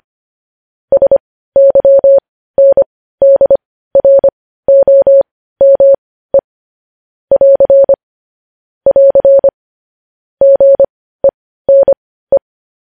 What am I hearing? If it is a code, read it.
Morse code, heard as a keyed tone, about 25 wpm, spaced as for 15 wpm, 550 Hz. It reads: SYNDROME <AR> <AR> GENE
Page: news headlines in morse code at 15 wpm